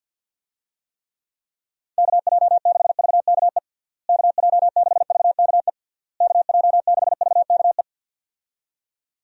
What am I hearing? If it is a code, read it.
Morse code, heard as a keyed tone, about 49 wpm, 700 Hz. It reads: KJ6VKE KJ6VKE KJ6VKE
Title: USA Call Signs - 6 Characters - Repeated 3x - 50wpm